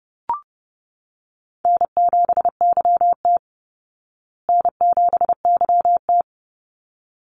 Morse code, heard as a keyed tone, about 30 wpm, 700 Hz. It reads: N7YT N7YT
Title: Call Signs Repeated 3x - 30wpm